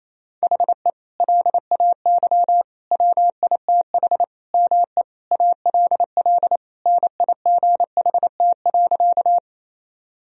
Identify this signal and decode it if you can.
Morse code, heard as a keyed tone, about 28 wpm, 690 Hz. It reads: HE LAY WITH ME ALL NIGHT.
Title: Sentences from Top 1000 Words - Review Part 2 - 28wpm